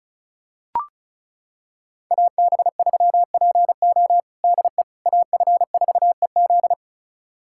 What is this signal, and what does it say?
Morse code, heard as a keyed tone, about 36 wpm, 700 Hz. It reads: AB3PO DE AF4EZ
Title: USA Callsign DE Callsign - 5 Characters - 35wpm